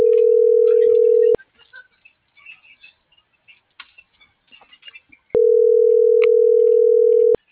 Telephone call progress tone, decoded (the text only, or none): ringback tone